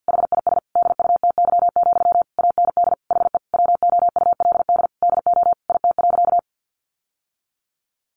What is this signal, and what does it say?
Morse code, heard as a keyed tone, about 50 wpm, 700 Hz. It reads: HIS DUTY, AND HE WOULD DO IT.
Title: Sentences from Top 1500 Words 50wpm